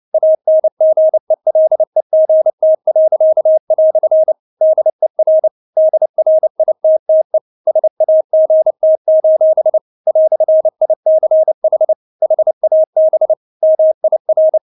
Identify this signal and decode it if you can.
Morse code, heard as a keyed tone, about 29 wpm, 630 Hz. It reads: ANGELEGT." DER DRITTE SAGT: "ICH HAB MIR
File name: Witz10 29wpm